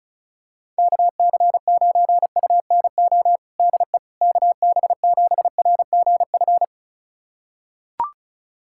Morse code, heard as a keyed tone, about 34 wpm, 700 Hz. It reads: KC9UNO DE KB7RGF
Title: USA Callsign DE Callsign 35wpm